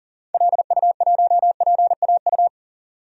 Morse code, heard as a keyed tone, about 40 wpm, 700 Hz. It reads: LU1PAU
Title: DXCC-100 Call Signs - 6 Characters - 40wpm